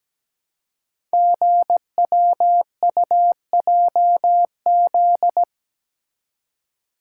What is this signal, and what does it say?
Morse code, heard as a keyed tone, about 17 wpm, 700 Hz. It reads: GWUJZ
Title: Code Groups - 5 Letters - 17wpm